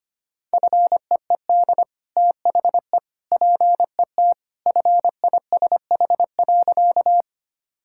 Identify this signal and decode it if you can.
Morse code, heard as a keyed tone, about 25 wpm, 690 Hz. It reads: FEED THE PET FISH.
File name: Easy Sentences with Hints - 25wpm